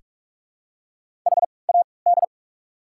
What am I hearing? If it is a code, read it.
Morse code, heard as a keyed tone, about 42 wpm, spaced as for 25 wpm, 700 Hz. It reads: HAD
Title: 3x Character Spacing - Top 300 Words - 45wpm